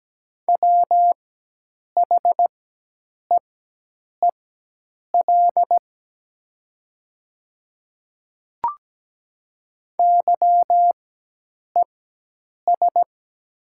Morse code, heard as a keyed tone, about 17 wpm, spaced as for 8 wpm, 700 Hz. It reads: WHEEL YES
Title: 4x Character Spacing - Top 500 Words - 17wpm